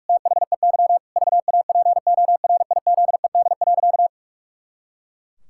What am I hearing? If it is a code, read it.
Morse code, heard as a keyed tone, about 41 wpm, 700 Hz. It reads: THEY VAPORIZED.
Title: Pangram Challenge v2 - 45wpm